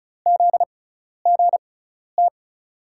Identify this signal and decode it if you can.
Morse code, heard as a keyed tone, about 35 wpm, spaced as for 12 wpm, 700 Hz. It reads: ZGT
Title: USA Call Signs - 5 Characters - 6x Character Spacing - 35wpm